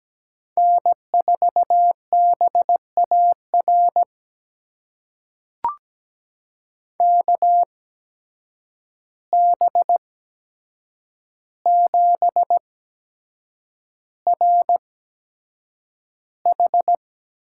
Morse code, heard as a keyed tone, about 17 wpm, spaced as for 7 wpm, 700 Hz. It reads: N4BAR KB7RH
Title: USA Call Signs - 8x Character Spacing - 17wpm